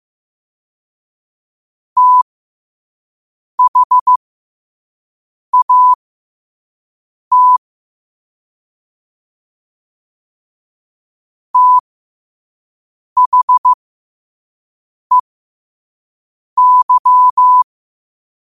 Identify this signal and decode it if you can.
Morse code, heard as a keyed tone, about 15 wpm, spaced as for 5 wpm, 1000 Hz. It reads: THAT THEY
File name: qotdfiles / 5wpm